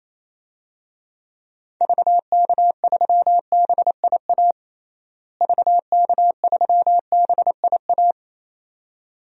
Morse code, heard as a keyed tone, about 28 wpm, 700 Hz. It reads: VK3BIA VK3BIA
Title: DXCC-100 Call Signs - 6 Characters - Repeated 2x - 28wpm